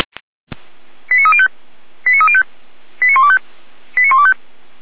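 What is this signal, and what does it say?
Five-tone selective call (ZVEI), heard as quick runs of five tones, tones of about 70 ms. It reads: 89305 89305 89125 89125
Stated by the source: Five-tone ZVEI selective call received on an SDR.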